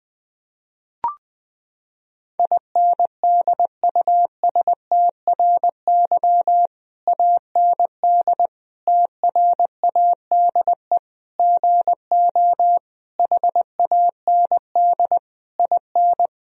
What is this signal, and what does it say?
Morse code, heard as a keyed tone, about 20 wpm, 700 Hz. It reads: INDUSTRY AND TRADE GO HAND IN
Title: Sentences from Top 1000 Words - Review Part 4 - 20wpm